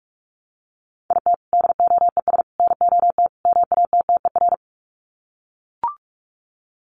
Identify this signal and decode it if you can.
Morse code, heard as a keyed tone, about 44 wpm, 700 Hz. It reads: IT DOES NOT MATTER
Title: Sets of 4 Words 45wpm